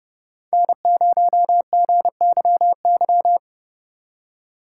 Morse code, heard as a keyed tone, about 35 wpm, 700 Hz. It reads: N0GYY
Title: Speed Racing - USA 1x3 Call Signs 30wpm